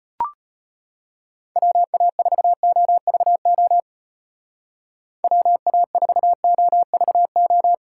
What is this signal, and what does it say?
Morse code, heard as a keyed tone, about 32 wpm, 700 Hz. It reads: WA4OVO WA4OVO